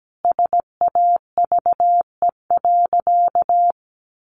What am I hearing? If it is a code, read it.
Morse code, heard as a keyed tone, about 17 wpm, 700 Hz. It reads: SAVE.
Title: Sentences from Top 1100 Words 17wpm